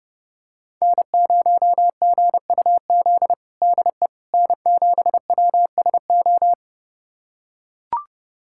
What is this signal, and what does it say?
Morse code, heard as a keyed tone, about 31 wpm, 700 Hz. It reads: N0GUZ DE N7WSO
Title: USA Callsign DE Callsign - 5 Characters - 30wpm